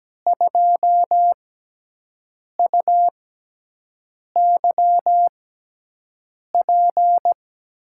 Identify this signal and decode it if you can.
Morse code, heard as a keyed tone, about 17 wpm, spaced as for 6 wpm, 700 Hz. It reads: 2UYP
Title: USA Call Signs - 5 Characters - 6x Character Spacing - 17wpm